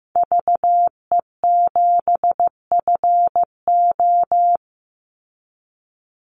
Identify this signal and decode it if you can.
Morse code, heard as a keyed tone, about 15 wpm, 700 Hz. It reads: VE7FO
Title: Call Signs Repeated 3x - 15wpm